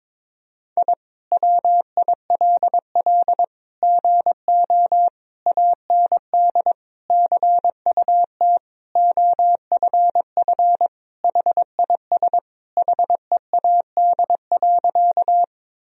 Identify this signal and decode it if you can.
Morse code, heard as a keyed tone, about 22 wpm, 700 Hz. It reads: I WILL GO AND CUT OFF HIS HEAD.
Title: Sentences from Top 1000 Words - Review Part 1 - 22wpm